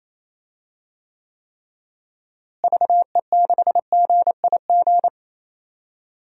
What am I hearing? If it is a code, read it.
Morse code, heard as a keyed tone, about 28 wpm, 700 Hz. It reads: VE6GIG
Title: DXCC-100 Call Signs - 6 Characters - Repeated 3x - 28wpm